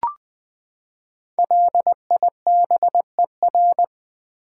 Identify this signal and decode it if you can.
Morse code, heard as a keyed tone, about 20 wpm, 700 Hz. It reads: LIBER
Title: Build A Word - 5 Letters At A Time - 10-Letter Words - 20wpm